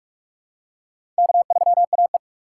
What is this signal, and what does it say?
Morse code, heard as a keyed tone, about 42 wpm, 700 Hz. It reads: K3AE
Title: USA Call Signs - 4 Characters - 4x Character Spacing - 45wpm